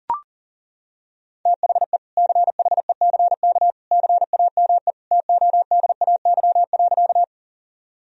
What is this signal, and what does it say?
Morse code, heard as a keyed tone, about 40 wpm, 700 Hz. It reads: THE CHECK CAME TODAY.
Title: Sentences from Top 1000 Words - Review Part 3 - 40wpm